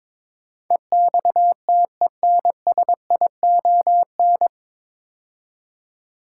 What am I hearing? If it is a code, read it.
Morse code, heard as a keyed tone, about 22 wpm, 700 Hz. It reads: EXTENSION
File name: 9 Letter Words - First 3 Letters Given - 22wpm